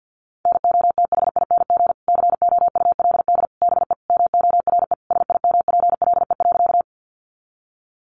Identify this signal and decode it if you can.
Morse code, heard as a keyed tone, about 50 wpm, 700 Hz. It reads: NOTHING COULD BE MORE SIMPLE.
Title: Sentences from Top 500 Words 50wpm